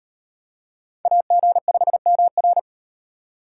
Morse code, heard as a keyed tone, about 35 wpm, 700 Hz. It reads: AG5MR